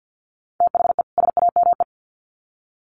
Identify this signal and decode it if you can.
Morse code, heard as a keyed tone, about 50 wpm, 700 Hz. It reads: THE SAME
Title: Sets of 2 Words 50wpm